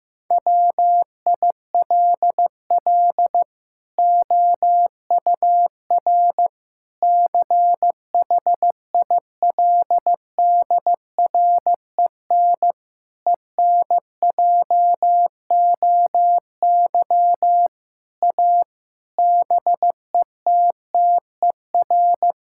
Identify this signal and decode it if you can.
Morse code, heard as a keyed tone, about 15 wpm, 700 Hz. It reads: WILL OUR CHILDREN ENJOY A BETTER
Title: Sentences from Top 1000 Words - Review Part 5 - 15wpm